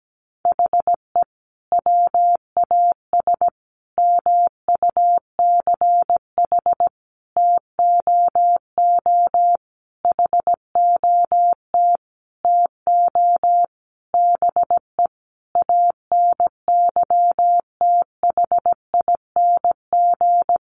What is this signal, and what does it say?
Morse code, heard as a keyed tone, about 17 wpm, 700 Hz. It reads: HE WAS MUCH TOO HOT TO BE ANYTHING
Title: Sentences from Top 1200 Words 17wpm